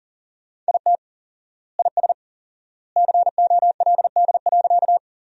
Morse code, heard as a keyed tone, about 39 wpm, 700 Hz. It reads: IT IS COLD.
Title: 4x Word-Spacing - Sentences from Top 400 Words - 40wpm